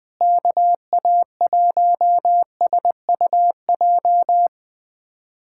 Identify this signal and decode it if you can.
Morse code, heard as a keyed tone, about 20 wpm, 700 Hz. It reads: KA1SUJ
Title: USA Callsign DE Callsign 20wpm